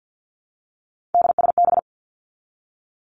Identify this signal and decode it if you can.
Morse code, heard as a keyed tone, about 50 wpm, 700 Hz. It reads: DSB
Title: Three Challenging Letter-Number Combinations 50wpm